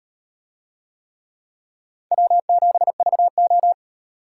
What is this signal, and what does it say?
Morse code, heard as a keyed tone, about 38 wpm, 700 Hz. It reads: W7VO